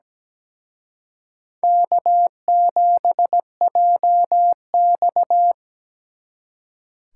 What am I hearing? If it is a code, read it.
Morse code, heard as a keyed tone, about 20 wpm, 700 Hz. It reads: K7JX
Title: Speed Racing - USA 1x2 Call Signs 17wpm